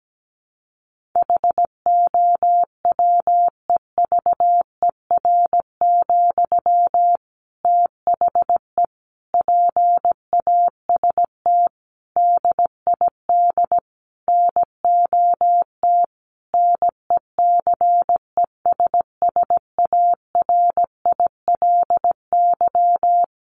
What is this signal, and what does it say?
Morse code, heard as a keyed tone, about 17 wpm, 700 Hz. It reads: HOWEVER, THE PAST DID NOT NECESSARILY
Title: Sentences from Top 1400 Words 17wpm